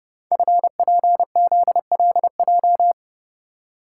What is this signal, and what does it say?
Morse code, heard as a keyed tone, about 32 wpm, 700 Hz. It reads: FPZLJ
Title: Code Groups - 5 Letter:Numbers - 30wpm